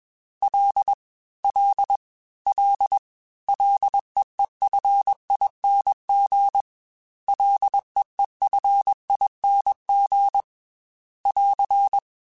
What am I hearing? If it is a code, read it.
Morse code, heard as a keyed tone, about 21 wpm, 800 Hz. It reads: L L L LEEFING LEEFING <AR>
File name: F and L / F and L mp3 / F and L Prefixes 3x2 Twice @20